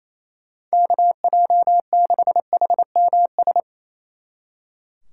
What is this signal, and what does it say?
Morse code, heard as a keyed tone, about 33 wpm, 700 Hz. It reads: KJ6HMS
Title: Speed Racing - USA 2x3 Call Signs 28wpm